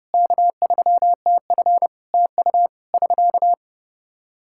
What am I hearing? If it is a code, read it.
Morse code, heard as a keyed tone, about 30 wpm, 700 Hz. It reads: K3TF TU <SK>
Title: Practice QSO - Exchange 4 - Ending - 30wpm